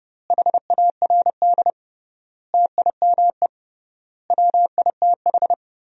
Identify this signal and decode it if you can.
Morse code, heard as a keyed tone, about 32 wpm, 700 Hz. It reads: HARD TIME WITH
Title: 3x Word-Spacing - Sentences from Top 300 Words - 30wpm